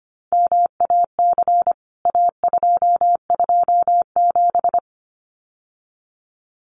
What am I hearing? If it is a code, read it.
Morse code, heard as a keyed tone, about 25 wpm, 700 Hz. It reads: MAC A227
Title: CWT Exchange 25wpm